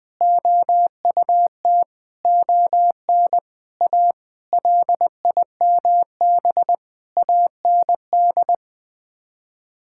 Morse code, heard as a keyed tone, about 21 wpm, 700 Hz. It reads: OUT ON A LIMB AND
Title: Sets of 8 Words - 20wpm